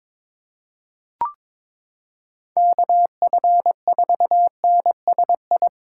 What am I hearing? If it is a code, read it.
Morse code, heard as a keyed tone, about 22 wpm, 700 Hz. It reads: KF4NSI
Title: USA Callsign DE Callsign - 6 Characters - 22wpm